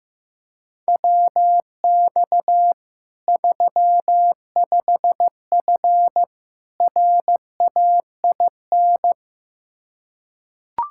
Morse code, heard as a keyed tone, about 15 wpm, 700 Hz. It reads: WX 35F RAIN
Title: K1USN SST QSO Practice 15wpm